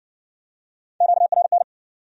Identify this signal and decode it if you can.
Morse code, heard as a keyed tone, about 56 wpm, 700 Hz. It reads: QRN